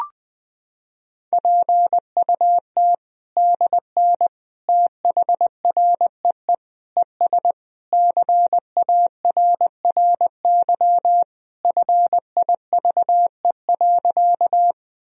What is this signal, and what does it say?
Morse code, heard as a keyed tone, about 21 wpm, 700 Hz. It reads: PUT DN THREE ES CARRY FIVE.
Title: Sentences from Top 400 Words and Abbreviations 20wpm